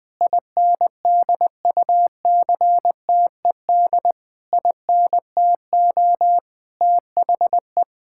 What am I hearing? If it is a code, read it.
Morse code, heard as a keyed tone, about 20 wpm, 700 Hz. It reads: INDUCTED INTO THE
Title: Sentences with 6-Word Phrases - 20wpm